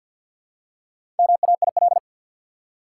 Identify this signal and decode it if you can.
Morse code, heard as a keyed tone, about 53 wpm, 700 Hz. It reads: MAIL
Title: Four Letter Words - 50wpm